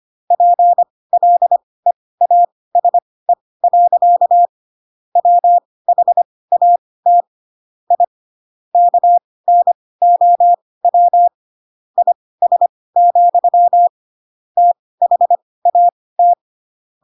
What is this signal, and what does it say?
Morse code, heard as a keyed tone, about 25 wpm, spaced as for 18 wpm, 700 Hz. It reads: PLEASE. WHAT I KNOW IS, THAT
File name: Pinocchio / Pinocchio ch4 12 25 ik1hge